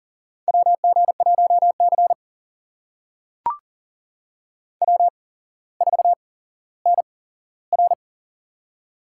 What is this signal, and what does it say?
Morse code, heard as a keyed tone, about 39 wpm, spaced as for 17 wpm, 700 Hz. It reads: WG1C W4NR
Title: USA Call Signs - 4 Characters - 8x Character Spacing - 40wpm